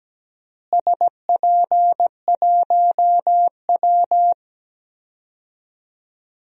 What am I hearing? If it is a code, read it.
Morse code, heard as a keyed tone, about 17 wpm, 700 Hz. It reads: SP1W
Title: DXCC-100 Call Signs - 4 Characters - Repeated 3x - 17wpm